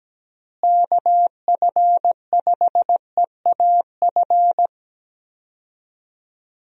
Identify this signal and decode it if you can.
Morse code, heard as a keyed tone, about 17 wpm, 700 Hz. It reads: KF5EAF
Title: USA Call Signs - 6 Characters - Repeated 2x - 17wpm